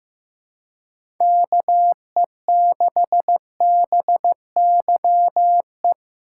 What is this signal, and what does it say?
Morse code, heard as a keyed tone, about 15 wpm, 700 Hz. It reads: KE6BYE
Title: USA Call Signs - 6 Characters - 3x Character Spacing - 15wpm